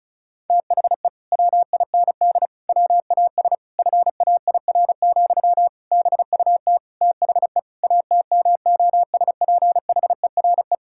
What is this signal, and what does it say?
Morse code, heard as a keyed tone, about 36 wpm, 700 Hz. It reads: THE WIND WAS FAIR, BUT THE ATMOSPHERE
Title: Sentences from Top 1600 Words 35wpm